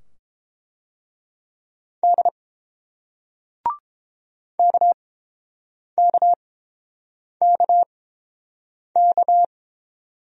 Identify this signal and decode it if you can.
Morse code, heard as a keyed tone, about 30 wpm, spaced as for 7 wpm, 700 Hz. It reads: D KKKK